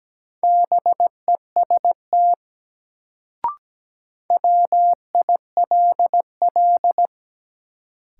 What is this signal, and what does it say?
Morse code, heard as a keyed tone, about 17 wpm, 700 Hz. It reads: BEST WILL
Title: Sets of 2 Words - New Word Drill - 17wpm